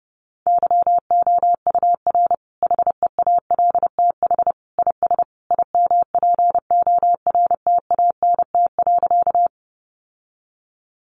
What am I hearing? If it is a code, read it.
Morse code, heard as a keyed tone, about 30 wpm, 700 Hz. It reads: YOUR HEALTH IS IMPORTANT.
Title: Sentences from Top 600 Words 30wpm